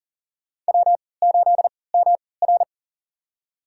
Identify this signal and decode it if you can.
Morse code, heard as a keyed tone, about 40 wpm, spaced as for 23 wpm, 700 Hz. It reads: W8MR
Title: USA Call Signs - 4 Characters - 3x Character Spacing - 40wpm